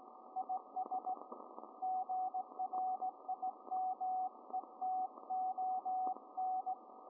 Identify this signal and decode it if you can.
Morse code, heard as a keyed tone, about 17 wpm, 750 Hz, with radio noise. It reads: IS GRIMETON